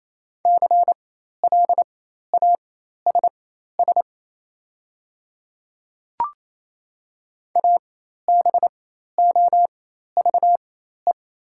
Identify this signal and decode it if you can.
Morse code, heard as a keyed tone, about 28 wpm, spaced as for 13 wpm, 700 Hz. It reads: CLASS ABOVE
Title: 4x Character Spacing - Top 500 Words - 28wpm